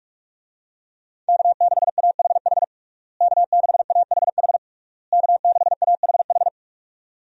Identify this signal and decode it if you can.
Morse code, heard as a keyed tone, about 43 wpm, 700 Hz. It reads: K6AHH K6AHH K6AHH
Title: USA Call Signs - 5 Characters - Repeated 3x - 45wpm